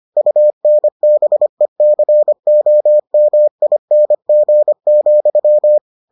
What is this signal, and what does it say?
Morse code, heard as a keyed tone, about 25 wpm, 600 Hz. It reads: UNBECOMING,